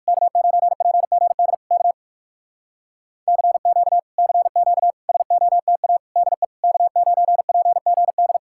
Morse code, heard as a keyed tone, about 59 wpm, 700 Hz. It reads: K9PGD K CQ CQ SOTA DE K9PGD